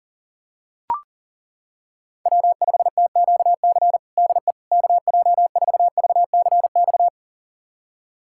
Morse code, heard as a keyed tone, about 40 wpm, 700 Hz. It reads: W5TQC DE KJ4VCX
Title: USA Callsign DE Callsign 40wpm